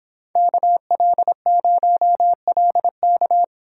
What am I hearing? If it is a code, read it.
Morse code, heard as a keyed tone, about 26 wpm, 700 Hz. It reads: KL0LK